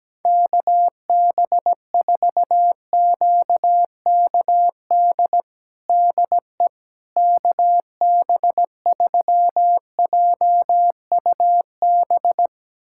Morse code, heard as a keyed tone, about 17 wpm, 700 Hz. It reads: KB4QKD DE KB3JUB
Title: USA Callsign DE Callsign - 6 Characters - 17wpm